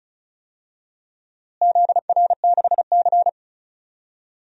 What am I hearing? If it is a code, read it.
Morse code, heard as a keyed tone, about 36 wpm, 700 Hz. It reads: ZR6C
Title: DXCC-100 Call Signs - 4 Characters - Repeated 3x - 35wpm